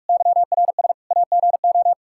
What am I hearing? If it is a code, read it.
Morse code, heard as a keyed tone, about 46 wpm, 700 Hz. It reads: YRS AGO
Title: Practice QSO - Exchange 3 - More Info - 45wpm